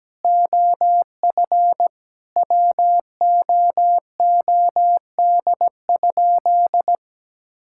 Morse code, heard as a keyed tone, about 17 wpm, 700 Hz. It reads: OF WOOD?
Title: Sentences from Top 1600 Words 17wpm